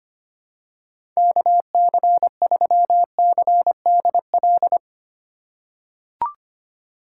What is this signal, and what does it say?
Morse code, heard as a keyed tone, about 25 wpm, 700 Hz. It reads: KC3CDL
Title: USA Call Signs - 3x Character Spacing - 25wpm